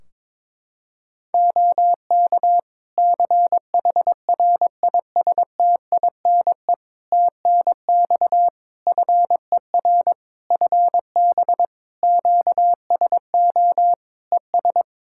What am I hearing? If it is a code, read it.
Morse code, heard as a keyed tone, about 21 wpm, 700 Hz. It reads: OK CHRISTINE TNX FER FB QSO ES
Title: Practice QSO - Exchange 4 - Ending - 22wpm